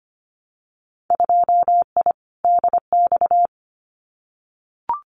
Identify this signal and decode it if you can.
Morse code, heard as a keyed tone, about 25 wpm, 700 Hz. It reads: 2I DX
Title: Winter Field Day Exchanges 25wpm